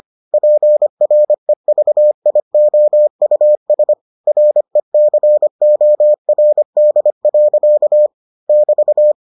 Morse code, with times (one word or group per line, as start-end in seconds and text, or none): 0.3-3.9 PREVIOUS
4.3-8.1 RECORD.
8.5-9.1 <BT>